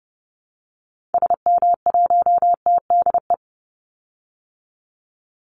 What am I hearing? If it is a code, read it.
Morse code, heard as a keyed tone, about 30 wpm, 700 Hz. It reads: SM1TDE
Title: Call Signs Repeated 2x - 30wpm